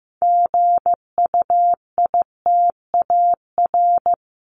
Morse code, heard as a keyed tone, about 15 wpm, 700 Hz. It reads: GUITAR
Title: Top 1501-1750 Words 15wpm